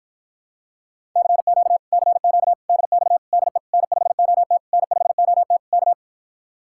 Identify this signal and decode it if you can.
Morse code, heard as a keyed tone, about 46 wpm, 700 Hz. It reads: CQ CQ DX DE N5OT N5OT K